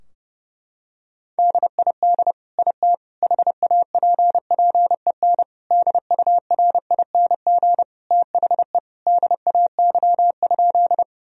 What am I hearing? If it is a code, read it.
Morse code, heard as a keyed tone, about 30 wpm, 700 Hz. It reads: DID IT HAPPEN DURING THE DAY?
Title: Sentences from Top 1000 Words - Review Part 2 - 30wpm